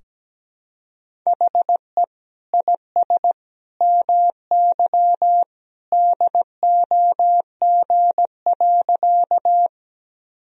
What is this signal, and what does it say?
Morse code, heard as a keyed tone, about 17 wpm, 700 Hz. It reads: HE IS MY DOG.
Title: Sentences from Top 1000 Words - Review Part 2 - 17wpm